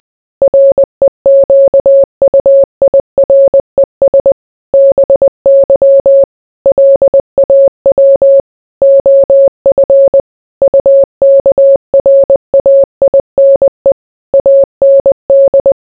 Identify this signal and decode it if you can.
Morse code, heard as a keyed tone, about 20 wpm, 550 Hz. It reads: REQUIRES BY LAW OF UKRAINE AND